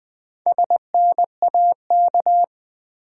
Morse code, heard as a keyed tone, about 20 wpm, 700 Hz. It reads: SNAK
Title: Build A Word - 1 Letter At A Time - 3 to 5-Letter Words - 20wpm